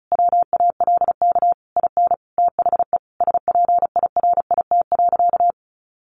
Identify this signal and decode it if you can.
Morse code, heard as a keyed tone, about 34 wpm, 700 Hz. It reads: WALK IN THE SPIRIT.
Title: Sentences from Top 1400 Words 35wpm